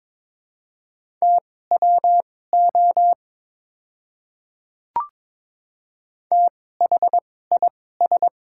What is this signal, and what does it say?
Morse code, heard as a keyed tone, about 21 wpm, spaced as for 16 wpm, 700 Hz. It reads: TWO THIS
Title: 2x Character Spacing - Top 100 Words - 22wpm